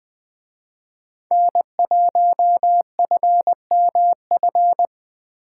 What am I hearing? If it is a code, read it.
Morse code, heard as a keyed tone, about 20 wpm, 700 Hz. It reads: N1FMF
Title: Speed Racing - USA Call Signs 17wpm